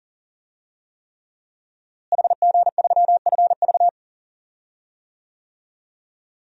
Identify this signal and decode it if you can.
Morse code, heard as a keyed tone, about 40 wpm, 700 Hz. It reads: HG3FV
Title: DXCC-100 Call Signs - 5 Characters - Repeated 3x - 40wpm